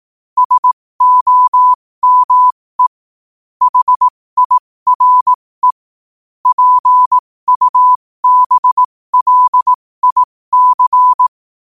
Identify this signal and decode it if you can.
Morse code, heard as a keyed tone, about 18 wpm, spaced as for 15 wpm, 1000 Hz. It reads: SOME HIRE PUBLIC
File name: qotdfiles / 15wpm